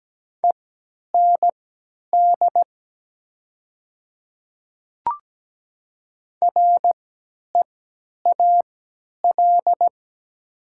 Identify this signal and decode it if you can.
Morse code, heard as a keyed tone, about 17 wpm, spaced as for 10 wpm, 700 Hz. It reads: END REAL
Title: 3x Character Spacing - Top 300 Words - 17wpm